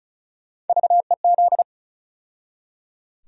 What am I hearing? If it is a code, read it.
Morse code, heard as a keyed tone, about 36 wpm, 700 Hz. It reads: VEZ
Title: USA Call Sign Postfixes - 3 Letters 35wpm